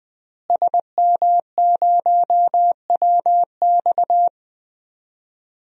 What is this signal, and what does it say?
Morse code, heard as a keyed tone, about 21 wpm, 700 Hz. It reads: SM0WX
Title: DXCC-100 Call Signs - 5 Characters - Repeated 2x - 20wpm